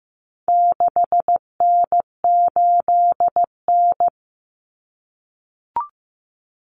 Morse code, heard as a keyed tone, about 15 wpm, 700 Hz. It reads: 6N8N
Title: Common Cut Numbers - 4-Digit 15wpm